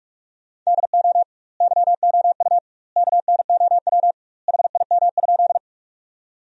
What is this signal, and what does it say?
Morse code, heard as a keyed tone, about 47 wpm, 700 Hz. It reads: DO YOU KNOW HIM?
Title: 2x Word-Spacing - Sentences from Top 100 Words - 45wpm